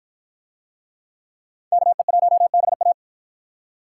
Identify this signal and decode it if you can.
Morse code, heard as a keyed tone, about 53 wpm, 700 Hz. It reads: KE1BA